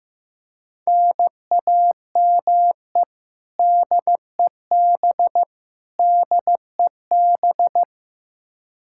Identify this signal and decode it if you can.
Morse code, heard as a keyed tone, about 15 wpm, 700 Hz. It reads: NAME DEB DEB
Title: Practice QSO - Exchange 1 - Beginning - 15wpm